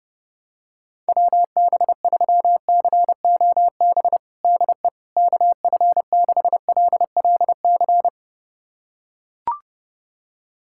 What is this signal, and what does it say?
Morse code, heard as a keyed tone, about 30 wpm, 700 Hz. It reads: WB3COB DE KF6LLC
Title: USA Callsign DE Callsign - 6 Characters - 30wpm